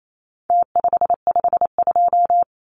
Morse code, heard as a keyed tone, about 28 wpm, 700 Hz. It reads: T552
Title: Numbers - Contest Style - 4-Digit 28wpm